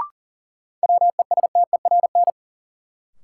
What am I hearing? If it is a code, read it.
Morse code, heard as a keyed tone, about 40 wpm, 700 Hz. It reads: WESTERN
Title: Sentences from Top 1000 Words - Review Part 7 - New Word Drill - 40wpm